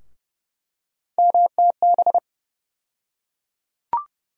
Morse code, heard as a keyed tone, about 30 wpm, 700 Hz. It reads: MTB
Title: USA Call Sign Postfixes - 3 Letters 30wpm